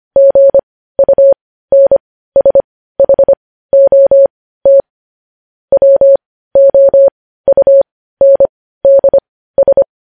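Morse code, heard as a keyed tone, about 25 wpm, spaced as for 15 wpm, 550 Hz. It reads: GUNSHOT WOUNDS
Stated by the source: News headlines sent in Morse code at 15 words per minute, preceded by vvv.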